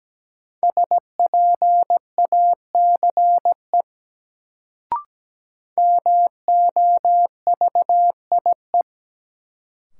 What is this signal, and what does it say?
Morse code, heard as a keyed tone, about 17 wpm, 700 Hz. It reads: SPACE MOVIE
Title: Sentences from Top 1000 Words - Review Part 5 - New Word Drill - 17wpm